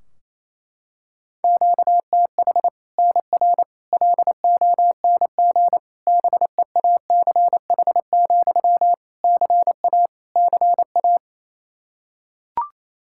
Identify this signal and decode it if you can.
Morse code, heard as a keyed tone, about 28 wpm, 700 Hz. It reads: QTH NR LONG BEACH, CA CA
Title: K1USN SST QSO Practice 28wpm